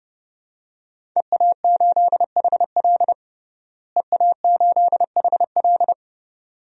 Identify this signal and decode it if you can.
Morse code, heard as a keyed tone, about 30 wpm, 700 Hz. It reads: EA8HL EA8HL
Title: DXCC-100 Call Signs - 5 Characters - Repeated 2x - 30wpm